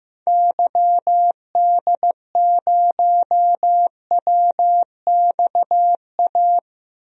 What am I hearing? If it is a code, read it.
Morse code, heard as a keyed tone, about 15 wpm, 700 Hz. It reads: YD0WXA
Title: DXCC-100 Call Signs - 6 Characters - 15wpm